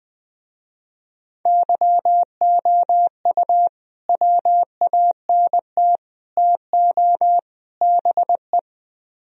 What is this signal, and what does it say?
Morse code, heard as a keyed tone, about 20 wpm, 700 Hz. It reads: YOU WANT TO BE
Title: Speed-Racing - Sets of 4 Words 17wpm